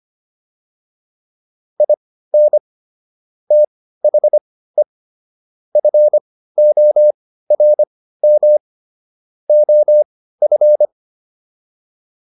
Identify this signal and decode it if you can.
Morse code, heard as a keyed tone, about 25 wpm, spaced as for 15 wpm, 600 Hz. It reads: IN THE FORM OF
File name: WD305 15